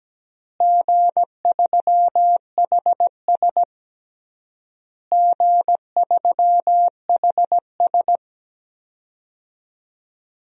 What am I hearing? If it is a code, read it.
Morse code, heard as a keyed tone, about 17 wpm, 700 Hz. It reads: G3HS G3HS
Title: DXCC-100 Call Signs - 4 Characters - Repeated 3x - 17wpm